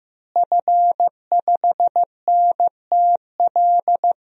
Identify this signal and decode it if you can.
Morse code, heard as a keyed tone, about 16 wpm, 700 Hz. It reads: F5NTL
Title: DXCC-100 Call Signs - 5 Characters - Repeated 2x - 15wpm